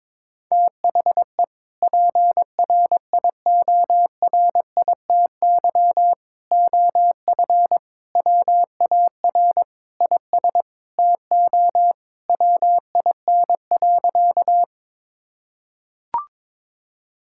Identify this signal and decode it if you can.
Morse code, heard as a keyed tone, about 22 wpm, 700 Hz. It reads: THE PRIORITY OF WAR IS TO WIN.
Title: Sentences from Top 1600 Words 22wpm